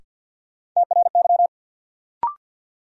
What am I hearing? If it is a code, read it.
Morse code, heard as a keyed tone, about 53 wpm, 700 Hz. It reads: TRY
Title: Sentences from Top 200 Words - New Word Drill - 50wpm